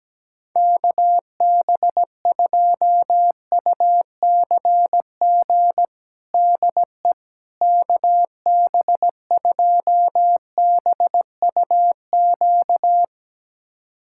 Morse code, heard as a keyed tone, about 17 wpm, 700 Hz. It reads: KB2UCG DE KB2BUQ
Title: USA Callsign DE Callsign - 6 Characters - 17wpm